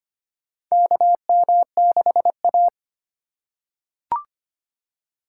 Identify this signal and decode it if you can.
Morse code, heard as a keyed tone, about 25 wpm, 700 Hz. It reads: KM6A
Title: USA Call Signs - 6x Character Spacing - 25wpm